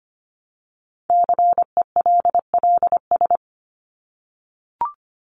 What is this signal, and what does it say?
Morse code, heard as a keyed tone, about 25 wpm, 700 Hz. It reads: CELLS
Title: ES:S Suffix - Top 100 Words with Base Words 25wpm